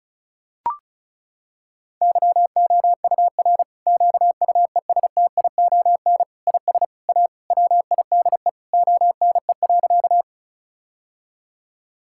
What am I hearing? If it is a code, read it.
Morse code, heard as a keyed tone, about 36 wpm, 700 Hz. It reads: YOUR QUESTION IS A WIDE ONE.
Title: Sentences from Top 1000 Words - Review Part 6 - 35wpm